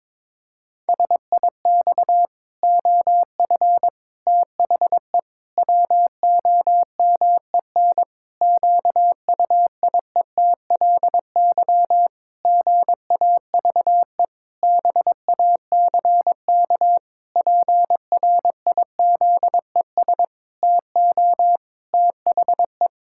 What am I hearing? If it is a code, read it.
Morse code, heard as a keyed tone, about 22 wpm, 700 Hz. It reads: SIX OF THE WOMEN QUIETLY GAVE BACK PRIZES TO THE
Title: Pangram Challenge v2 - 22wpm